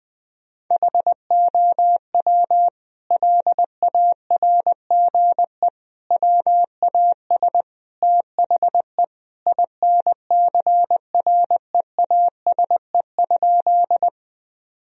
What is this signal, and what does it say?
Morse code, heard as a keyed tone, about 20 wpm, 700 Hz. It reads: HOW LARGE WAS THE INCREASE?
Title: Sentences from Top 1000 Words - Review Part 4 - 20wpm